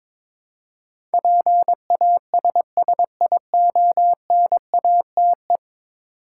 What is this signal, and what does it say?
Morse code, heard as a keyed tone, about 22 wpm, 700 Hz. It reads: PASSIONATE
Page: morse-code-ninja-practice-22wpm